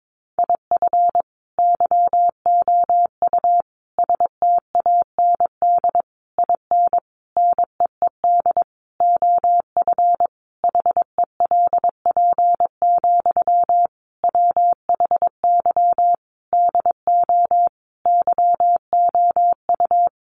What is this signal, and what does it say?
Morse code, heard as a keyed tone, about 22 wpm, 700 Hz. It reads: IF YOU STAND IN NEED OF HELP, WHY DO YOU
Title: Sentences from Top 1200 Words 22wpm